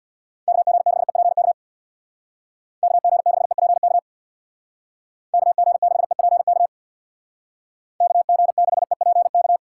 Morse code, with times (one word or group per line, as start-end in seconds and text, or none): 0.5-1.5 KG6EPK
2.8-4.0 KG6EPK
5.3-6.7 KG6EPK
8.0-9.6 KG6EPK